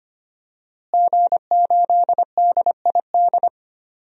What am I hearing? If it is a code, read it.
Morse code, heard as a keyed tone, about 25 wpm, 700 Hz. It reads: G8DID
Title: DXCC-100 Call Signs - 5 Characters - Repeated 3x - 25wpm